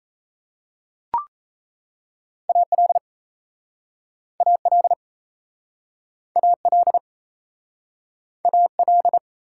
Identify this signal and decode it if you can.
Morse code, heard as a keyed tone, about 32 wpm, 700 Hz. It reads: AL AL AL AL
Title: Speed-Racing Prefixes and Suffixes 28wpm